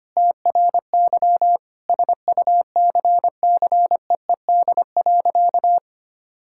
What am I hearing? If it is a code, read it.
Morse code, heard as a keyed tone, about 25 wpm, 700 Hz. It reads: TRY SUCCEED.
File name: practice / 25wpm / Sentences with 2-Word Phrases - 25wpm